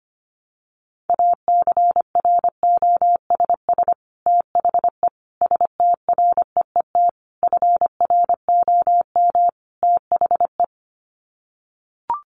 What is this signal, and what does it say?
Morse code, heard as a keyed tone, about 25 wpm, 700 Hz. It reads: ACROSS THE STREET FROM THE
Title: Sets of 5 Words Encore 25wpm